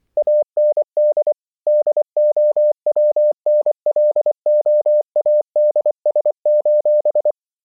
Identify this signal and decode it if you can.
Morse code, heard as a keyed tone, about 24 wpm, 600 Hz. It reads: AND DOWNLOADS: